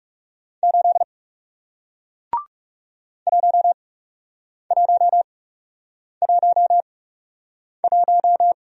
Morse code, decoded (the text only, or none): 8 1111